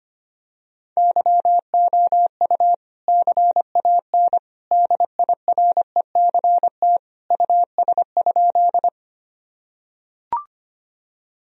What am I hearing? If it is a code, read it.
Morse code, heard as a keyed tone, about 25 wpm, 700 Hz. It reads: YOU CAN DIRECT US?
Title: Sentences from Top 1000 Words - Review Part 2 - 25wpm